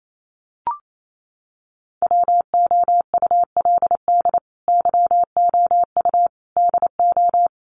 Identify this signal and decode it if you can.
Morse code, heard as a keyed tone, about 28 wpm, 700 Hz. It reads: WOULD YOU DO
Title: Sentences from Top 100 Words 28wpm